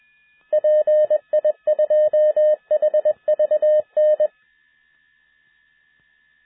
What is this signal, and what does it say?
Morse code, heard as a keyed tone, about 21 wpm, 610 Hz. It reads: PI2HVN